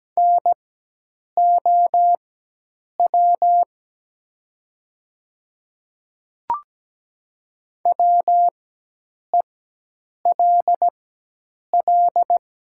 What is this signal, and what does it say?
Morse code, heard as a keyed tone, about 17 wpm, spaced as for 8 wpm, 700 Hz. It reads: NOW WELL
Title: 4x Character Spacing - Top 300 Words - 17wpm